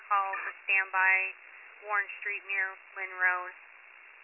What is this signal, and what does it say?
Emergency Alert System (SAME) tones, heard as harsh buzzing bursts, with speech. It is heard from 0.9 to 3.4 s.